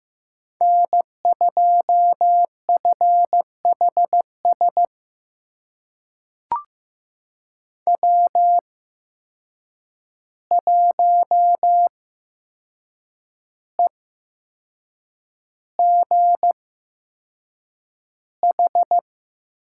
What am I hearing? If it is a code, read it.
Morse code, heard as a keyed tone, about 15 wpm, spaced as for 6 wpm, 700 Hz. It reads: N2FHS W1EGH
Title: USA Call Signs - 5 Characters - 8x Character Spacing - 15wpm